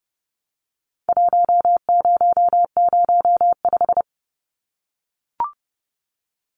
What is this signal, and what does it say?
Morse code, heard as a keyed tone, about 30 wpm, 700 Hz. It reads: 1005
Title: Numbers - Contest Style - 4-Digit 30wpm